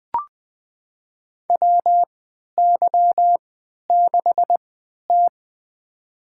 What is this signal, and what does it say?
Morse code, heard as a keyed tone, about 20 wpm, spaced as for 11 wpm, 700 Hz. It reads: WY6T
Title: USA Call Signs - 4 Characters - 3x Character Spacing - 20wpm